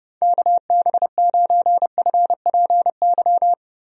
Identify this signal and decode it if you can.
Morse code, heard as a keyed tone, about 30 wpm, 700 Hz. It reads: KB9FPY
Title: USA Call Signs - 6 Characters - Repeated 3x - 30wpm